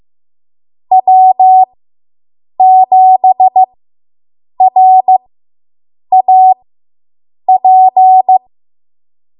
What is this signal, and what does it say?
Morse code, heard as a keyed tone, about 15 wpm, spaced as for 7 wpm, 750 Hz. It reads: W7RAP